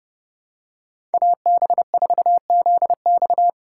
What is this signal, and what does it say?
Morse code, heard as a keyed tone, about 30 wpm, 700 Hz. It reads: AB4ZX
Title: USA Call Signs - 5 Characters - 4x Character Spacing - 30wpm